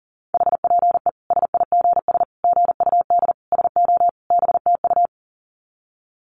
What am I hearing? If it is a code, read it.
Morse code, heard as a keyed tone, about 37 wpm, 700 Hz. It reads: HPE SIGS GUD SO BTU
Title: Common QSO Phrases 40wpm